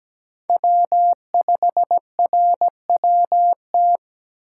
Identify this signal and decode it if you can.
Morse code, heard as a keyed tone, about 17 wpm, 700 Hz. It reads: W5RWT
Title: USA Call Signs - 5 Characters - 3x Character Spacing - 17wpm